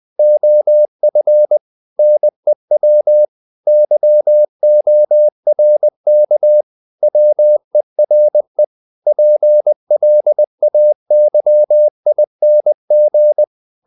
Morse code, heard as a keyed tone, about 20 wpm, 600 Hz. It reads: OF NEW YORK WERE PLAYING
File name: the old man and the sea / the old man and the sea part 1 20wpm